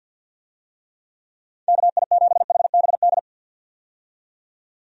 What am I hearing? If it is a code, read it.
Morse code, heard as a keyed tone, about 51 wpm, 700 Hz. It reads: KI7HBD
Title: USA Call Signs - 6 Characters - 50wpm